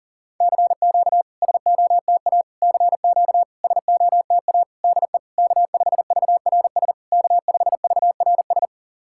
Morse code, heard as a keyed tone, about 40 wpm, 700 Hz. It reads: CQ SOTA CQ SOTA DE K5VRS K5VRS
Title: SOTA QSO Practice 40wpm